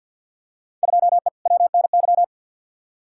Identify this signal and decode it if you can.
Morse code, heard as a keyed tone, about 47 wpm, 700 Hz. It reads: 2E WNY
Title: Field Day Exchanges Encore - 50wpm